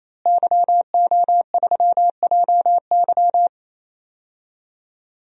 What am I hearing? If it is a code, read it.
Morse code, heard as a keyed tone, about 28 wpm, 700 Hz. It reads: YO3JY
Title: DXCC-100 Call Signs - 5 Characters - Repeated 3x - 28wpm